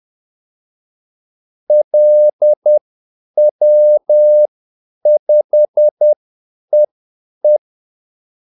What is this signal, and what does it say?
Morse code, heard as a keyed tone, about 10 wpm, spaced as for 8 wpm, 600 Hz. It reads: LW5EE
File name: Callsigns-10-8wpm